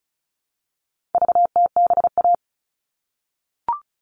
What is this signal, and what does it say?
Morse code, heard as a keyed tone, about 34 wpm, 700 Hz. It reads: VTBA
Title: All Cut Numbers - 4-Digit 35wpm